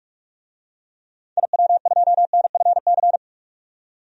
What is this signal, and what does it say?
Morse code, heard as a keyed tone, about 44 wpm, 690 Hz. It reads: IW2NFC
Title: DXCC-100 Call Signs - 6 Characters - 45wpm